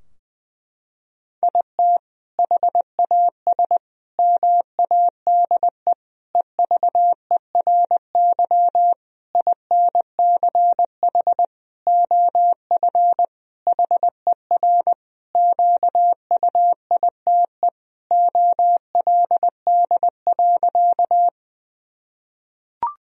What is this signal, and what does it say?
Morse code, heard as a keyed tone, about 20 wpm, 700 Hz. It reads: IT HAS MADE EVERY INCH OF HER QUITE OLD.
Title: Sentences from Top 1000 Words - Review Part 3 - 20wpm